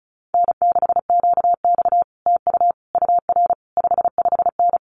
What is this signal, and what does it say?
Morse code, heard as a keyed tone, about 34 wpm, 700 Hz. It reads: N6QX TU UR 55N
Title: Common QSO Phrases Encore 35wpm